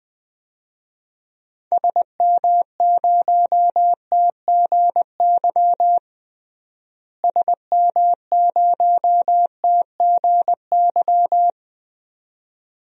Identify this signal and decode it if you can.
Morse code, heard as a keyed tone, about 20 wpm, 700 Hz. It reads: SM0TGY SM0TGY
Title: DXCC-100 Call Signs - 6 Characters - Repeated 2x - 20wpm